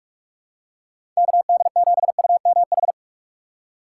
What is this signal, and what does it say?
Morse code, heard as a keyed tone, about 43 wpm, 700 Hz. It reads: KD7UMH
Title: USA Call Signs - 6 Characters - 4x Character Spacing - 45wpm